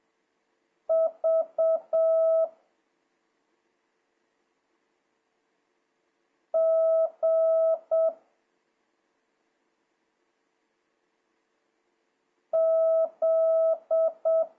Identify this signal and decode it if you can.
Morse code, heard as a keyed tone, about 7 wpm, 1300 Hz. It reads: V G Z
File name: jh8cbh / 2026cw